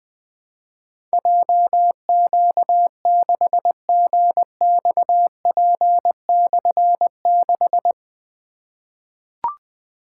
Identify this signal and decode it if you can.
Morse code, heard as a keyed tone, about 21 wpm, 700 Hz. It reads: JQ6GXP/6
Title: Difficult Call Signs 20wpm